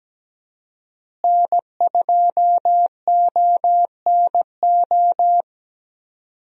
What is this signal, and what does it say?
Morse code, heard as a keyed tone, about 17 wpm, 700 Hz. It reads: N2ONO
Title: USA Call Signs - 5 Characters - Repeated 3x - 17wpm